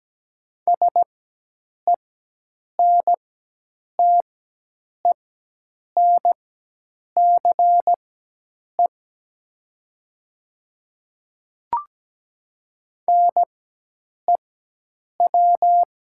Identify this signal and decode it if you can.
Morse code, heard as a keyed tone, about 17 wpm, spaced as for 8 wpm, 700 Hz. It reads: SENTENCE NEW
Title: 4x Character Spacing - Top 200 Words - 17wpm